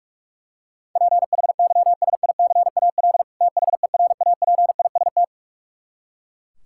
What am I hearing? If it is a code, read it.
Morse code, heard as a keyed tone, about 42 wpm, 700 Hz. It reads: PHYSICAL THERAPIST
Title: K1USN SST QSO Practice 45wpm